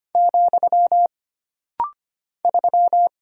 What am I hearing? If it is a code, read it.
Morse code, heard as a keyed tone, about 25 wpm, 700 Hz. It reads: , 3
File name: Single Character (Letter:Number:Punctuation:Prosign) - Rapid-Fire - 25wpm